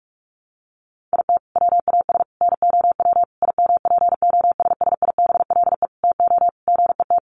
Morse code, heard as a keyed tone, about 44 wpm, 700 Hz. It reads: IT WAS NOW IMPOSSIBLE TO GET
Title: Sentences from Top 1100 Words 45wpm